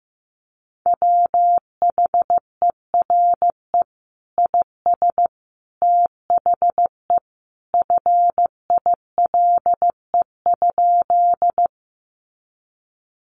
Morse code, heard as a keyed tone, about 15 wpm, 700 Hz. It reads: WHERE IS THE FILE?
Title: Sentences from Top 1200 Words 15wpm